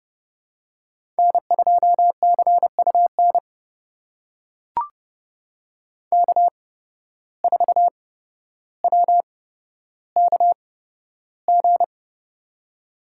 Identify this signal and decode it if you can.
Morse code, heard as a keyed tone, about 30 wpm, spaced as for 13 wpm, 700 Hz. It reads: N2CUN K4WKG
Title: USA Call Signs - 5 Characters - 8x Character Spacing - 30wpm